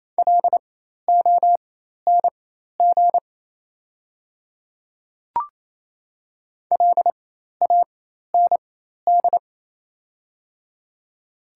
Morse code, heard as a keyed tone, about 27 wpm, spaced as for 13 wpm, 690 Hz. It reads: LONG LAND
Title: 4x Character Spacing - Top 300 Words - 28wpm